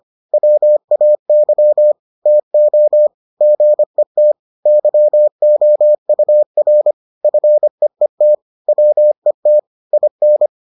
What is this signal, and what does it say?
Morse code, heard as a keyed tone, about 25 wpm, 600 Hz. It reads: WAY TO GET YOUR FEET WET IN